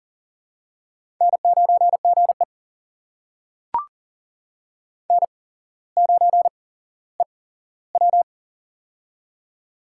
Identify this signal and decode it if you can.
Morse code, heard as a keyed tone, about 39 wpm, spaced as for 17 wpm, 700 Hz. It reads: N9GE N9EW
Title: USA Call Signs - 4 Characters - 8x Character Spacing - 40wpm